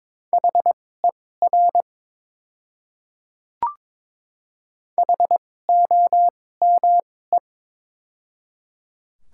Morse code, heard as a keyed tone, about 22 wpm, spaced as for 16 wpm, 700 Hz. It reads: HER HOME
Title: 2x Character Spacing - Top 200 Words - 22wpm